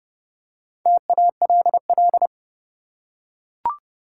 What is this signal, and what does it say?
Morse code, heard as a keyed tone, about 31 wpm, 700 Hz. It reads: TALL
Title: Four Letter Words - 30wpm